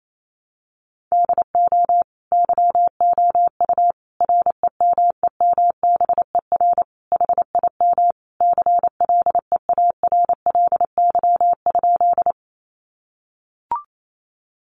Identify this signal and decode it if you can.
Morse code, heard as a keyed tone, about 28 wpm, 700 Hz. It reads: DO YOU REMEMBER HIM CLEARLY?
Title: Sentences from Top 1300 Words 28wpm